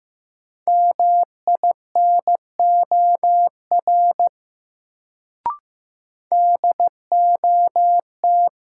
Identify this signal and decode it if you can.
Morse code, heard as a keyed tone, about 15 wpm, 700 Hz. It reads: MINOR DOT
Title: Sentences from Top 1500 Words - New Word Drill - 15wpm